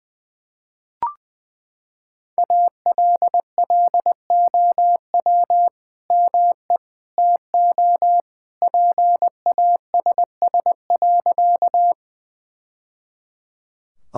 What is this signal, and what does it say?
Morse code, heard as a keyed tone, about 20 wpm, 700 Hz. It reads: ALLOW ME TO PASS.
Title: Sentences from Top 1000 Words - Review Part 5 - 20wpm